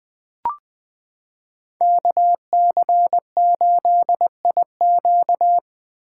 Morse code, heard as a keyed tone, about 20 wpm, 700 Hz. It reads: KC8IQ
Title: 2023 POTA Hunter Call Signs - 20wpm